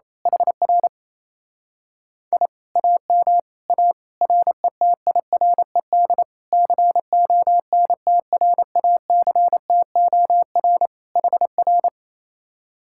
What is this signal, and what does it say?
Morse code, heard as a keyed tone, about 27 wpm, 700 Hz. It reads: HR I AM A RETIRED CONTRACTOR HR
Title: Speed-Racing Common QSO Phrases Encore - 28wpm